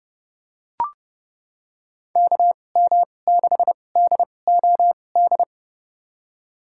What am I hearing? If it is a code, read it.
Morse code, heard as a keyed tone, about 30 wpm, spaced as for 22 wpm, 700 Hz. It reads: KM6DOD
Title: USA Call Signs - 2x Character Spacing - 30wpm